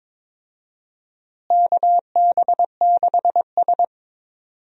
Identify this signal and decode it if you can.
Morse code, heard as a keyed tone, about 21 wpm, 700 Hz. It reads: KB6S
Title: USA Call Signs - 4 Characters - Repeated 3x - 22wpm